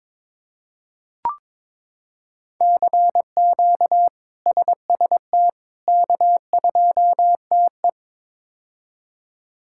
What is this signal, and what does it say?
Morse code, heard as a keyed tone, about 22 wpm, 700 Hz. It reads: CQ SST K2TE
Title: K1USN SST QSO Practice - 22wpm